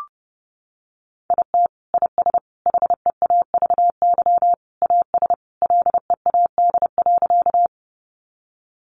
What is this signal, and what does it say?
Morse code, heard as a keyed tone, about 30 wpm, 700 Hz. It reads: IT IS HEAVY AS LEAD.
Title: Sentences from Top 1100 Words 30wpm